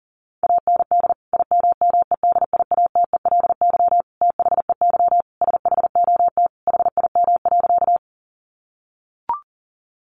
Morse code, heard as a keyed tone, about 40 wpm, 700 Hz. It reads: AND IMMEDIATELY THEY SHOT HIM.
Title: Sentences from Top 1100 Words 40wpm